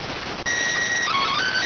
SSTV (slow-scan television), PD 290.